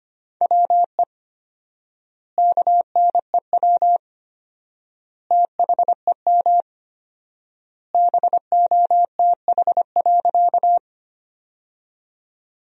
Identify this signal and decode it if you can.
Morse code, heard as a keyed tone, about 25 wpm, 700 Hz. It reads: WE KNEW THEM BOTH.
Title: 4x Word-Spacing - Sentences from Top 400 Words - 25wpm